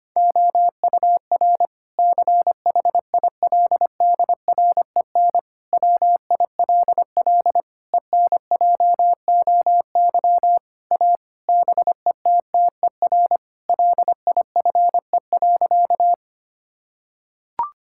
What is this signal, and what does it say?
Morse code, heard as a keyed tone, about 25 wpm, 700 Hz. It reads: OUR CHILDREN WILL ENJOY A BETTER LIFE.
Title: Sentences from Top 1000 Words - Review Part 5 - 25wpm